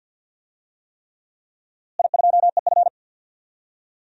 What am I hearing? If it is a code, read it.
Morse code, heard as a keyed tone, about 50 wpm, 700 Hz. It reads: I2EF
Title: DXCC-100 Call Signs - 4 Characters - Repeated 3x - 50wpm